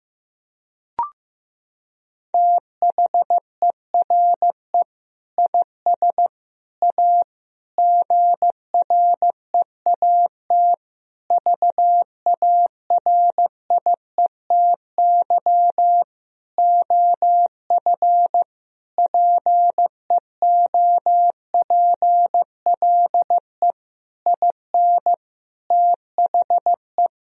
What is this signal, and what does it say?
Morse code, heard as a keyed tone, about 15 wpm, 700 Hz. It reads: THERE IS A GREAT VARIETY OF PEOPLE IN THE
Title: Sentences from Top 1000 Words - Review Part 4 - 15wpm